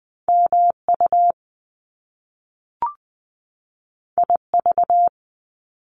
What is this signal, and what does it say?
Morse code, heard as a keyed tone, about 20 wpm, 700 Hz. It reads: MU IV
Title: Top Two Letter Combinations in Words 20wpm